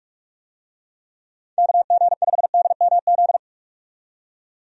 Morse code, heard as a keyed tone, about 41 wpm, 700 Hz. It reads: KG5DMZ
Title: USA Call Signs - 6 Characters - 45wpm